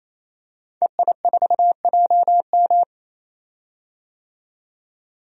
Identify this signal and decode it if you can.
Morse code, heard as a keyed tone, about 28 wpm, 700 Hz. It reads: EI4JM
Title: DXCC-100 Call Signs - 5 Characters - 28wpm